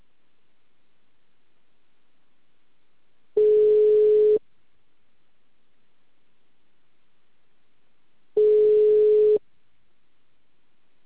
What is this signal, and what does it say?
Ringback tone (Europe, 425 Hz).